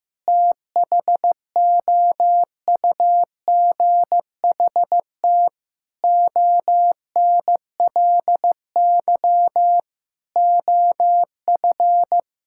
Morse code, heard as a keyed tone, about 16 wpm, 700 Hz. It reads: THOUGHT ONLY OF
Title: Sentences from Top 1600 Words 15wpm